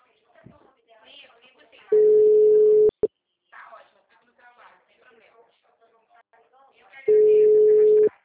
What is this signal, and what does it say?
Ringback tone (Europe, 425 Hz), with speech.
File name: out- -2005- - - .4976